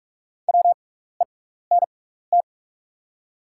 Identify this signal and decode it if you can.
Morse code, heard as a keyed tone, about 44 wpm, spaced as for 15 wpm, 690 Hz. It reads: WENT
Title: 6x Character Spacing - Top 300 Words - 45wpm